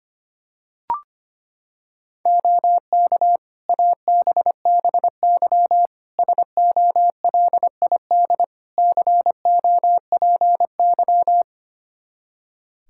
Morse code, heard as a keyed tone, about 25 wpm, 700 Hz. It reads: OK ABBY SOLID COPY
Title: Practice QSO - Exchange 3 - More Info - 25wpm